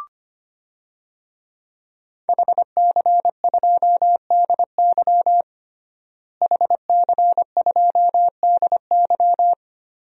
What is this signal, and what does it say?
Morse code, heard as a keyed tone, about 25 wpm, 700 Hz. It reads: HC2DY HC2DY
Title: DXCC-100 Call Signs - 5 Characters - Repeated 3x - 25wpm